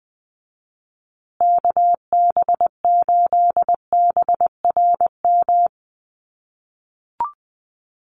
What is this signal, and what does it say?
Morse code, heard as a keyed tone, about 20 wpm, 700 Hz. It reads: KB8BRM
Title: USA Call Signs Repeated 2x 20wpm